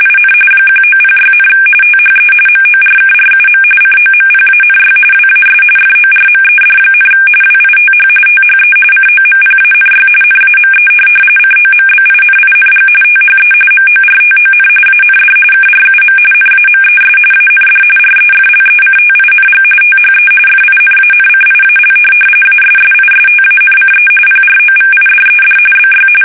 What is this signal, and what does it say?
RTTY, about 45.45 baud. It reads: IN 1906, THE MORKRUM COMPANY WAS FORMED, WITH THE COMPANY NAME COMBINING THE MORTON AND KRUM NAMES AND REFLECTING THE FINANCIAL ASSISTANCE PROVIDED BY
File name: fsk850 / teletype-corp